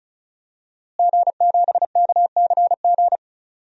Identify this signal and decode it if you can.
Morse code, heard as a keyed tone, about 36 wpm, 700 Hz. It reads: G7KCG
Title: DXCC-100 Call Signs - 5 Characters - Repeated 3x - 35wpm